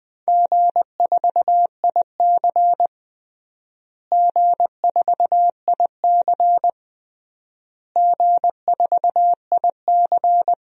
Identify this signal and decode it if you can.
Morse code, heard as a keyed tone, about 20 wpm, 700 Hz. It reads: G4IC G4IC G4IC
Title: DXCC-100 Call Signs - 4 Characters - Repeated 3x - 20wpm